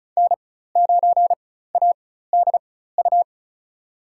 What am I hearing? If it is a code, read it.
Morse code, heard as a keyed tone, about 36 wpm, spaced as for 16 wpm, 700 Hz. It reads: N9ADU
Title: USA Call Signs - 5 Characters - 4x Character Spacing - 35wpm